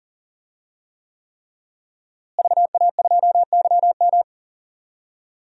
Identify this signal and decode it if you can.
Morse code, heard as a keyed tone, about 40 wpm, 700 Hz. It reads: VA2YM
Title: DXCC-100 Call Signs - 5 Characters - Repeated 3x - 40wpm